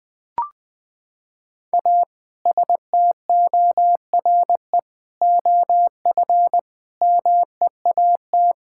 Morse code, heard as a keyed tone, about 20 wpm, 700 Hz. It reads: A STORE OF MEAT
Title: Sentences from Top 1000 Words - Review Part 5 - 20wpm